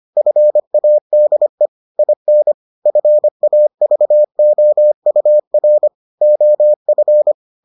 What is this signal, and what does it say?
Morse code, heard as a keyed tone, about 25 wpm, 600 Hz. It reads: FADE IN FAVOUR OF